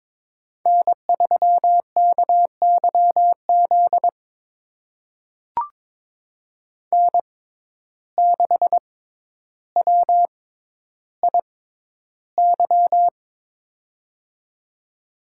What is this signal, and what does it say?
Morse code, heard as a keyed tone, about 22 wpm, spaced as for 11 wpm, 700 Hz. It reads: N3KYZ N6WIY
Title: USA Call Signs - 5 Characters - 6x Character Spacing - 22wpm